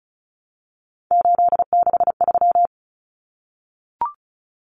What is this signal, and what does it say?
Morse code, heard as a keyed tone, about 34 wpm, 700 Hz. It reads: 863
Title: Common Cut Numbers - 3-Digit 35wpm